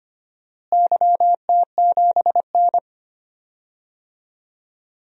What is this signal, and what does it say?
Morse code, heard as a keyed tone, about 25 wpm, 700 Hz. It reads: YT7N
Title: DXCC-100 Call Signs - 4 Characters - Repeated 3x - 25wpm